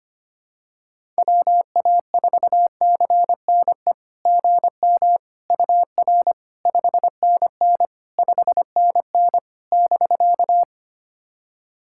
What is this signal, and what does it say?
Morse code, heard as a keyed tone, about 25 wpm, 690 Hz. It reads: WA4CNE GM UR 5NN 5NN <BK>
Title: SOTA QSO Practice 25wpm